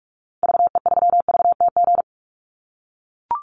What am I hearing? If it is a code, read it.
Morse code, heard as a keyed tone, about 44 wpm, 700 Hz. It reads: VE3VTG
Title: Call Signs - 45wpm